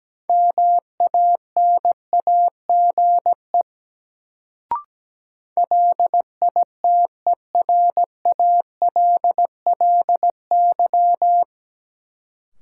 Morse code, heard as a keyed tone, about 17 wpm, 700 Hz. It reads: MANAGE LITERALLY
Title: Sentences from Top 1400 Words - New Word Drill Alt - 17wpm